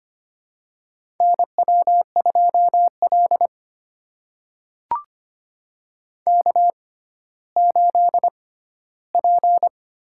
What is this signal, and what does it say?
Morse code, heard as a keyed tone, about 25 wpm, spaced as for 14 wpm, 700 Hz. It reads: NW2L K8P
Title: USA Call Signs - 4 Characters - 6x Character Spacing - 25wpm